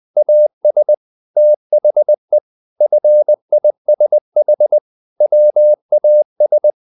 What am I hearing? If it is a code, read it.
Morse code, heard as a keyed tone, about 20 wpm, 600 Hz. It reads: AS THE FISH WAS
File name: the old man and the sea part 2 20wpm